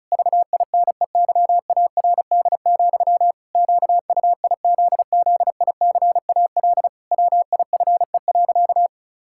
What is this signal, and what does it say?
Morse code, heard as a keyed tone, about 34 wpm, 700 Hz. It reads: VINEYARD, QUIZZICAL WIFE.
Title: Pangram Challenge v2 - 35wpm